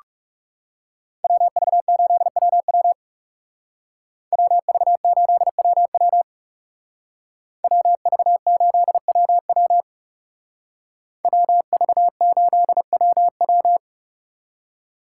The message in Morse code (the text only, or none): WV8WW WV8WW WV8WW WV8WW